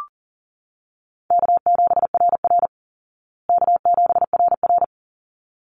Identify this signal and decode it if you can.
Morse code, heard as a keyed tone, about 40 wpm, 700 Hz. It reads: K7RR K7RR
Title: Easy Call Signs Repeated 2x - 40wpm